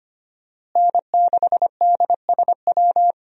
Morse code, heard as a keyed tone, about 25 wpm, 700 Hz. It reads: N6DSW
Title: USA Call Signs - 5 Characters - Repeated 3x - 25wpm